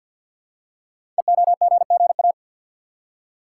Morse code, heard as a keyed tone, about 50 wpm, 700 Hz. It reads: EOGGA
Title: Code Groups - 5 Letters - 50wpm